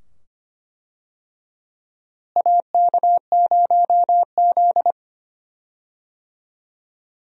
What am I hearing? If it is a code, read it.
Morse code, heard as a keyed tone, about 25 wpm, 700 Hz. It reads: AK0Z
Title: USA Call Signs - 4 Characters - Repeated 2x - 25wpm